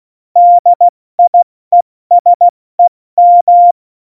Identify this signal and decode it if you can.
Morse code, heard as a keyed tone, about 16 wpm, spaced as for 14 wpm, 700 Hz. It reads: DIESEM
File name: Morsekurs / Witze / Witz 14 Juli 2020, 70 F80